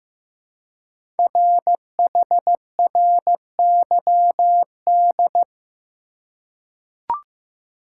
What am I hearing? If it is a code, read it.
Morse code, heard as a keyed tone, about 15 wpm, 700 Hz. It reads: RHRYD
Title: Code Groups - 5 Letters - 15wpm